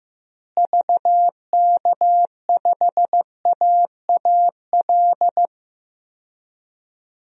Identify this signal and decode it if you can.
Morse code, heard as a keyed tone, about 15 wpm, 700 Hz. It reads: VK5AAL
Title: DXCC-100 Call Signs - 6 Characters - Repeated 2x - 15wpm